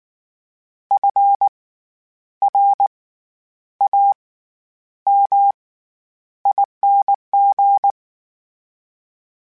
Morse code, heard as a keyed tone, about 19 wpm, spaced as for 9 wpm, 800 Hz. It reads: FRAMING
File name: Special INGs / Special INGs-mp3s / SlowToFaster ing @18